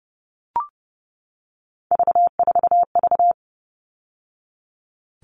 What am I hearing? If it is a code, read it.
Morse code, heard as a keyed tone, about 30 wpm, 700 Hz. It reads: V4V
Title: All Cut Numbers - 3-Digit 30wpm